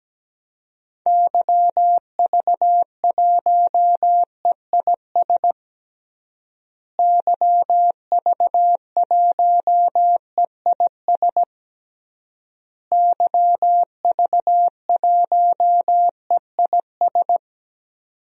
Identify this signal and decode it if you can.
Morse code, heard as a keyed tone, about 17 wpm, 700 Hz. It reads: YV1EIS YV1EIS YV1EIS
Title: DXCC-100 Call Signs - 6 Characters - Repeated 3x - 17wpm